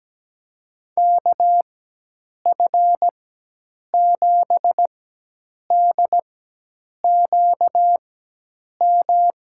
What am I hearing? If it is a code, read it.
Morse code, heard as a keyed tone, about 17 wpm, spaced as for 8 wpm, 700 Hz. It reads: KF7DQM
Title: USA Call Signs - 6 Characters - 4x Character Spacing - 17wpm